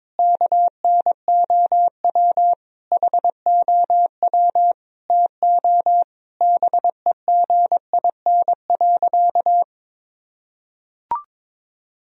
Morse code, heard as a keyed tone, about 22 wpm, 700 Hz. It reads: KNOW HOW TO BEGIN.
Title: Sentences from Top 1000 Words - Review Part 1 - 22wpm